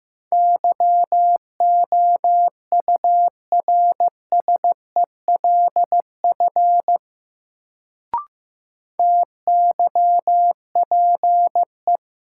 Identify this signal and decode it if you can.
Morse code, heard as a keyed tone, about 15 wpm, 700 Hz. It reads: YOURSELF TYPE
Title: Sentences from Top 600 Words - New Word Drill - 15wpm